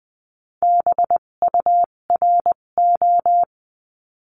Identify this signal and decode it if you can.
Morse code, heard as a keyed tone, about 20 wpm, spaced as for 17 wpm, 700 Hz. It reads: BURO
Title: 257 Lesson 42b - Words 14-20wpm